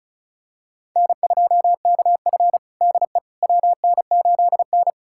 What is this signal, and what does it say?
Morse code, heard as a keyed tone, about 36 wpm, 700 Hz. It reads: N2KF DE WN8N
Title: USA Callsign DE Callsign - 4 Characters - 35wpm